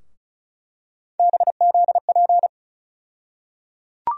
Morse code, heard as a keyed tone, about 36 wpm, 700 Hz. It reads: BZP
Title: USA Call Sign Postfixes - 3 Letters 35wpm